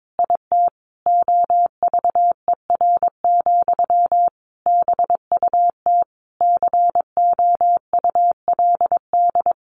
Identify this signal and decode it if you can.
Morse code, heard as a keyed tone, about 22 wpm, 700 Hz. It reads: IT OVER, BUT COULD
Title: Sentences from Top 1200 Words 22wpm